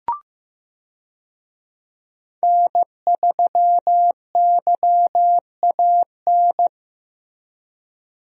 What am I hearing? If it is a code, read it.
Morse code, heard as a keyed tone, about 15 wpm, 700 Hz. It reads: N3YAN
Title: USA Call Signs - 5 Characters - Repeated 3x - 15wpm